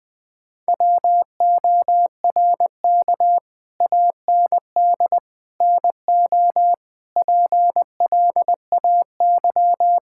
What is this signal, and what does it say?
Morse code, heard as a keyed tone, about 20 wpm, 700 Hz. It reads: WORK AND NO PLAY
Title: Sentences with 8-Word Phrases - 20wpm